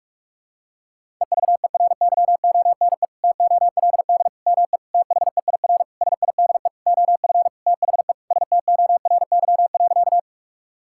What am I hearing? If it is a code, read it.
Morse code, heard as a keyed tone, about 46 wpm, 700 Hz. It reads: EVERYONE TOLD ME THEIR SIDE OF THE STORY.
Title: Sentences from Top 1000 Words - Review Part 5 - 45wpm